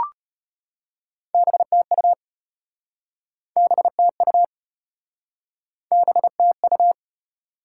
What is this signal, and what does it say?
Morse code, heard as a keyed tone, about 34 wpm, 700 Hz. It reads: BTU BTU BTU